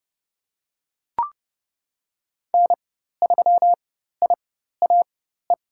Morse code, heard as a keyed tone, about 30 wpm, spaced as for 14 wpm, 700 Hz. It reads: N3IAE
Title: USA Call Signs - 5 Characters - 4x Character Spacing - 30wpm